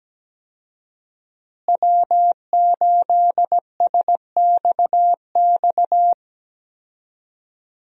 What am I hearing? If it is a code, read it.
Morse code, heard as a keyed tone, about 17 wpm, 700 Hz. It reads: W8SXX
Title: USA Call Signs - 5 Characters - Repeated 3x - 17wpm